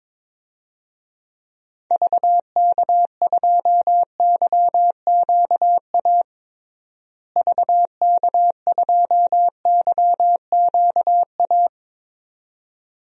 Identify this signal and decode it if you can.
Morse code, heard as a keyed tone, about 22 wpm, 700 Hz. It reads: VK2YQA VK2YQA
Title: DXCC-100 Call Signs - 6 Characters - Repeated 2x - 22wpm